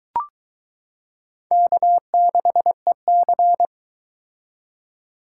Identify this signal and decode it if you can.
Morse code, heard as a keyed tone, about 22 wpm, 700 Hz. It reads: K6EC